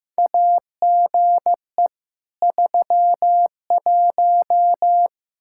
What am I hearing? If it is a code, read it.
Morse code, heard as a keyed tone, about 15 wpm, 700 Hz. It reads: AGE 31
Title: Practice QSO - Exchange 3 - More Info - 15wpm